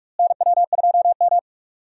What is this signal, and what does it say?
Morse code, heard as a keyed tone, about 47 wpm, 700 Hz. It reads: NW2M
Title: USA Call Signs - 4 Characters - Repeated 3x - 45wpm